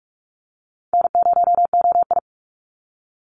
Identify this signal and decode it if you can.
Morse code, heard as a keyed tone, about 44 wpm, 700 Hz. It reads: N0OI
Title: NA SOTA Activator Callsigns 45wpm